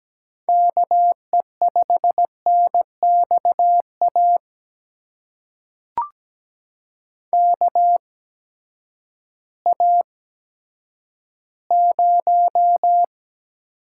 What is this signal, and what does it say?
Morse code, heard as a keyed tone, about 17 wpm, spaced as for 10 wpm, 700 Hz. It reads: KE5NXA KA0
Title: USA Call Signs - 6 Characters - 8x Character Spacing - 17wpm